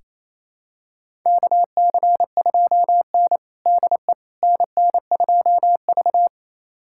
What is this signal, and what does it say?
Morse code, heard as a keyed tone, about 28 wpm, 700 Hz. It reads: KC2N DE NN2V
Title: USA Callsign DE Callsign - 4 Characters - 28wpm